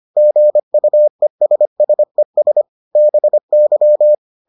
Morse code, heard as a keyed tone, about 25 wpm, 600 Hz. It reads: GUESSES BY